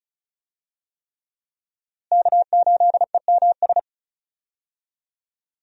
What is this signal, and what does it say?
Morse code, heard as a keyed tone, about 36 wpm, 700 Hz. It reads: K8EMS
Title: USA Call Signs - 5 Characters - Repeated 3x - 35wpm